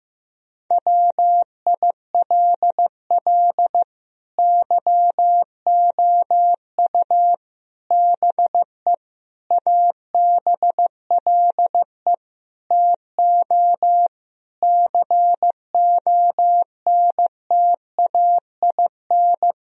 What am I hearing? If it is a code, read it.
Morse code, heard as a keyed tone, about 15 wpm, 700 Hz. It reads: WILL YOU BE ABLE TO CONTAIN
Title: Sentences from Top 1000 Words - Review Part 2 - 15wpm